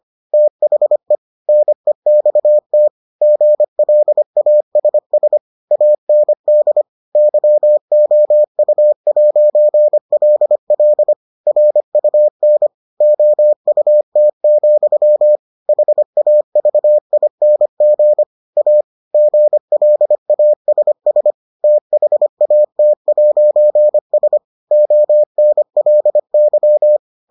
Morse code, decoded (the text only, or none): THE NEXT GLASS AND YOU'LL RUN OUT, HAVING A GLASS THAT'S ONLY